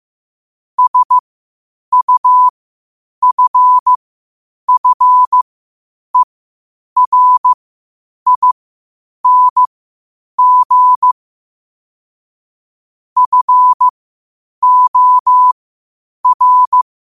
Morse code, heard as a keyed tone, about 15 wpm, spaced as for 8 wpm, 1000 Hz. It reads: SUFFERING FOR
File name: qotdfiles / 8wpm